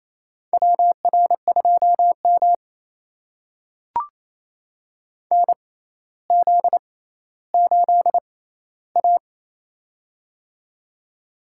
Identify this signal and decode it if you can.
Morse code, heard as a keyed tone, about 28 wpm, spaced as for 14 wpm, 700 Hz. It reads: WR2M NZ8A
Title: USA Call Signs - 4 Characters - 6x Character Spacing - 28wpm